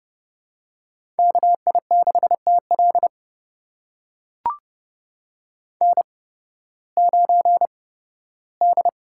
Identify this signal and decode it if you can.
Morse code, heard as a keyed tone, about 30 wpm, spaced as for 16 wpm, 700 Hz. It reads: KI6TL N9D
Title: USA Call Signs - 5 Characters - 8x Character Spacing - 30wpm